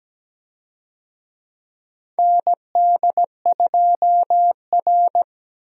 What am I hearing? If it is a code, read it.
Morse code, heard as a keyed tone, about 17 wpm, 700 Hz. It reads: ND2R
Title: USA Call Signs - 4 Characters - Repeated 3x - 17wpm